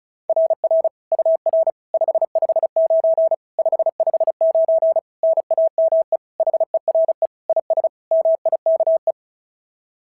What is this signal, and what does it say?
Morse code, heard as a keyed tone, about 34 wpm, 660 Hz. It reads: RR UR 559 559 NAME HERE IS MIKE
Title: SOTA QSO Practice 35wpm